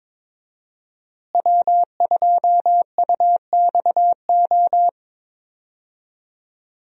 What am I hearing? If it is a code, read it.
Morse code, heard as a keyed tone, about 22 wpm, 700 Hz. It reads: W2UXO
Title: USA Call Signs - 5 Characters - Repeated 2x - 22wpm